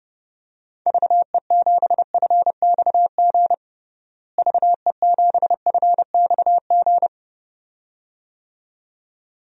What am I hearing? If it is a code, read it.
Morse code, heard as a keyed tone, about 30 wpm, 700 Hz. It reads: VE7FXG VE7FXG
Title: DXCC-100 Call Signs - 6 Characters - Repeated 2x - 30wpm